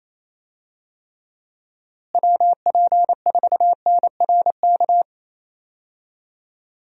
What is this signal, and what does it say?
Morse code, heard as a keyed tone, about 28 wpm, 700 Hz. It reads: WP4NRK
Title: USA Call Signs - 6 Characters - Repeated 3x - 28wpm